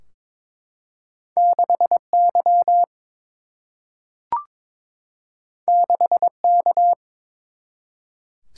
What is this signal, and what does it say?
Morse code, heard as a keyed tone, about 22 wpm, 700 Hz. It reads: 6 Y 6 K
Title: Single-Letter Single-Number 22wpm